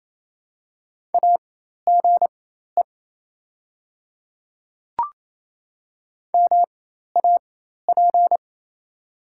Morse code, heard as a keyed tone, about 28 wpm, spaced as for 13 wpm, 700 Hz. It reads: AGE MAP
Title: 4x Character Spacing - Top 500 Words - 28wpm